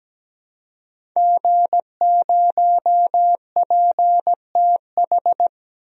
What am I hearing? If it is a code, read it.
Morse code, heard as a keyed tone, about 17 wpm, 700 Hz. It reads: G0PTH
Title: DXCC-100 Call Signs - 5 Characters - Repeated 2x - 17wpm